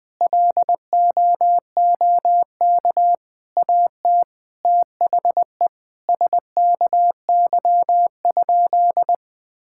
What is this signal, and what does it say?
Morse code, heard as a keyed tone, about 20 wpm, 700 Hz. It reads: LOOK AT THE SKY?
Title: Sentences with 4-Word Phrases - 20wpm